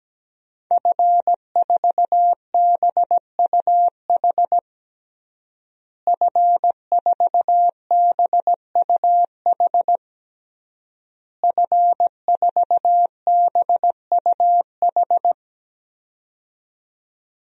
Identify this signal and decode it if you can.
Morse code, heard as a keyed tone, about 17 wpm, 700 Hz. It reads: F4BUH F4BUH F4BUH
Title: DXCC-100 Call Signs - 5 Characters - Repeated 3x - 17wpm